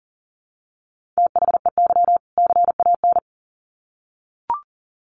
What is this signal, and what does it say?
Morse code, heard as a keyed tone, about 40 wpm, 700 Hz. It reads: THEY CAN
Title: Sets of 2 Words 40wpm